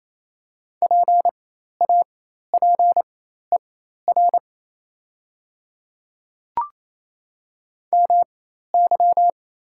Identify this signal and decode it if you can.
Morse code, heard as a keyed tone, about 28 wpm, spaced as for 13 wpm, 700 Hz. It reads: PAPER MY
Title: 4x Character Spacing - Top 300 Words - 28wpm